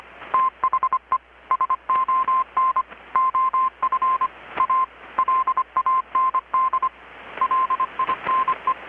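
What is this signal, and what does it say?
Morse code, heard as a keyed tone, about 25 wpm, 1040 Hz. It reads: THE SON OF A LAND LINE
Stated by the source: Morse from KPH's Night of Nights XX maritime radiotelegraphy event, received in Chiapas, Mexico about 3,800 km away on a Kenwood R-600 with a grounded 10 m long-wire antenna.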